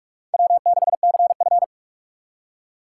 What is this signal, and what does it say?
Morse code, heard as a keyed tone, about 46 wpm, 700 Hz. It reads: W6CF
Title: USA Call Signs - 4 Characters - 4x Character Spacing - 45wpm